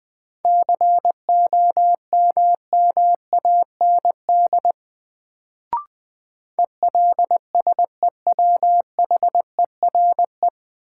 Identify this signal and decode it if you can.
Morse code, heard as a keyed tone, about 20 wpm, 700 Hz. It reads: COMMAND ELSEWHERE
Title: Sentences from Top 1600 Words - New Word Drill - 20wpm